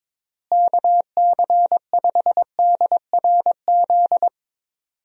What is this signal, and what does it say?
Morse code, heard as a keyed tone, about 21 wpm, 700 Hz. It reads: KC5DRZ
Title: USA Call Signs - 6 Characters - Repeated 2x - 22wpm